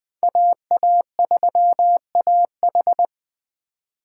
Morse code, heard as a keyed tone, about 20 wpm, 700 Hz. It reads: AA3AH
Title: Speed Racing - USA 2x2 Call Signs 17wpm